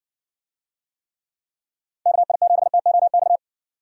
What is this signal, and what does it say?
Morse code, heard as a keyed tone, about 58 wpm, 700 Hz. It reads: KI7TOX